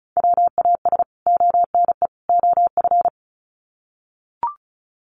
Morse code, heard as a keyed tone, about 34 wpm, 700 Hz. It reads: WAS ONE OF
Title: Sets of 3 Words 35wpm